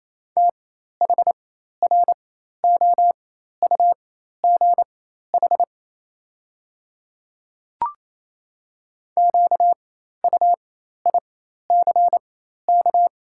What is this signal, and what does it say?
Morse code, heard as a keyed tone, about 28 wpm, spaced as for 13 wpm, 700 Hz. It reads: THROUGH QUICK
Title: 4x Character Spacing - Top 500 Words - 28wpm